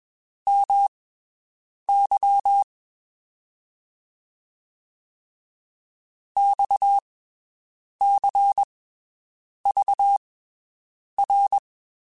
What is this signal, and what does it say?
Morse code, heard as a keyed tone, about 21 wpm, spaced as for 7 wpm, 800 Hz. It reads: MY XCVR
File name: (spaced) / QSOs Spaced-mp3s / WD0FBT 1 @20 sp